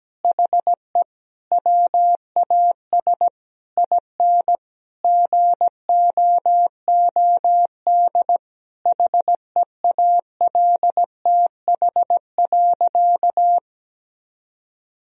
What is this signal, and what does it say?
Morse code, heard as a keyed tone, about 17 wpm, 700 Hz. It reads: HE WAS IN GOOD HEALTH.
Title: Sentences from Top 1000 Words - Review Part 3 - 17wpm